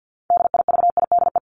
Morse code, heard as a keyed tone, about 50 wpm, 700 Hz. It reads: DIVIDE
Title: Top 1501-1750 Words 50wpm